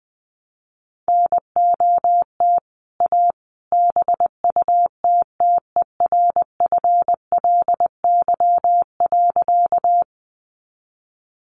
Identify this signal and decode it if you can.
Morse code, heard as a keyed tone, about 21 wpm, 700 Hz. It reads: NOT A BUTTERFLY.
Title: A Sound Of Thunder 20wpm